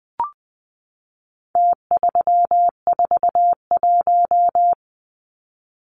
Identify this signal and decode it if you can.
Morse code, heard as a keyed tone, about 20 wpm, 700 Hz. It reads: T341
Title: Numbers - Contest Style - 4-Digit 20wpm